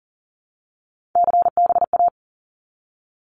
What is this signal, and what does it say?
Morse code, heard as a keyed tone, about 40 wpm, 700 Hz. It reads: CBA
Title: Abbreviations 40wpm